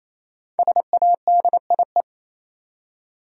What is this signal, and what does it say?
Morse code, heard as a keyed tone, about 28 wpm, 700 Hz. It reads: SADIE
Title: USA Common Names - 28wpm